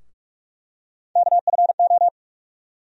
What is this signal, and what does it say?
Morse code, heard as a keyed tone, about 43 wpm, 700 Hz. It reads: KFO
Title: USA Call Sign Postfixes - 3 Letters 45wpm